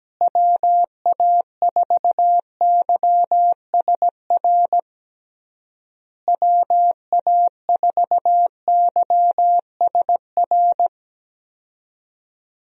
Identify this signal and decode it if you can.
Morse code, heard as a keyed tone, about 17 wpm, 700 Hz. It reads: WA4YSR WA4YSR
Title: USA Call Signs - 6 Characters - Repeated 3x - 17wpm